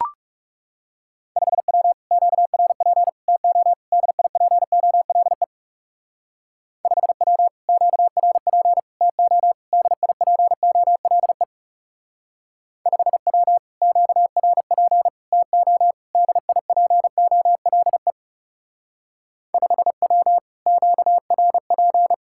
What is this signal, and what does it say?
Morse code, heard as a keyed tone, about 39 wpm, 700 Hz. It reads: 5W QRP TO DIPOLE 5W QRP TO DIPOLE 5W QRP TO DIPOLE 5W QRP